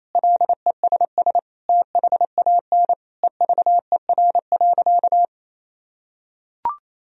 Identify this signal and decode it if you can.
Morse code, heard as a keyed tone, about 28 wpm, 700 Hz. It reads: LESS THAN EVER.
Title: Sentences from Top 1000 Words - Review Part 2 - 28wpm